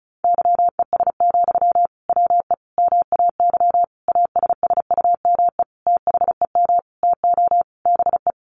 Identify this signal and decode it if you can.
Morse code, heard as a keyed tone, about 34 wpm, 700 Hz. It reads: YES, WE MAY ASSUME THEM TO BE
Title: Sentences from Top 1500 Words 35wpm